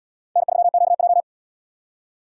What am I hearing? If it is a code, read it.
Morse code, heard as a keyed tone, about 77 wpm, 700 Hz. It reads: N3QJ